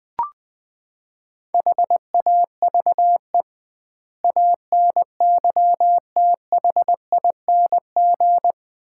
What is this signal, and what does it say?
Morse code, heard as a keyed tone, about 20 wpm, 700 Hz. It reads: HAVE ANYTHING
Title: 2x Word-Spacing - Sets of 5 Words - 20wpm